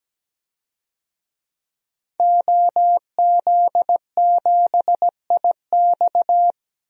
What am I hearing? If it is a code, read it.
Morse code, heard as a keyed tone, about 17 wpm, 700 Hz. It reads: OZ7IX
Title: DXCC-100 Call Signs - 5 Characters - Repeated 3x - 17wpm